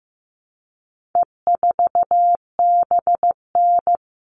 Morse code, heard as a keyed tone, about 15 wpm, 700 Hz. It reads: E4BN
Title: All Cut Numbers - 4-Digit 15wpm